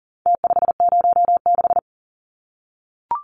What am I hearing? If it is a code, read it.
Morse code, heard as a keyed tone, about 40 wpm, 700 Hz. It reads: T506
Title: Numbers - Contest Style - 4-Digit 40wpm